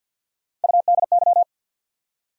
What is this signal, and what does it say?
Morse code, heard as a keyed tone, about 49 wpm, 700 Hz. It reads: UDY
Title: USA Call Sign Postfixes - 3 Letters 50wpm